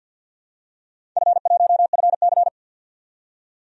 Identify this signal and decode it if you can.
Morse code, heard as a keyed tone, about 51 wpm, 700 Hz. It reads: F1FC
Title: DXCC-100 Call Signs - 4 Characters - 50wpm